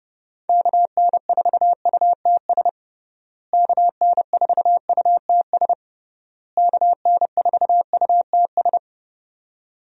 Morse code, heard as a keyed tone, about 30 wpm, 700 Hz. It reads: KN4UTS KN4UTS KN4UTS
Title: USA Call Signs - 6 Characters - Repeated 3x - 30wpm